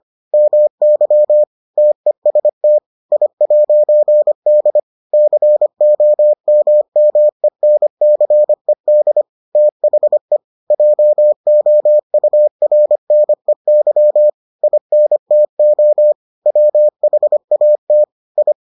Morse code, heard as a keyed tone, about 25 wpm, 600 Hz. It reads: MY TEST I'D COMMENCED THE JOURNEY INTO WHAT I